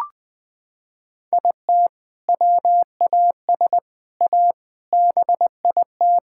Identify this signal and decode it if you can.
Morse code, heard as a keyed tone, about 21 wpm, 700 Hz. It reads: IT WAS A BIT
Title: Sets of 6 Words - 20wpm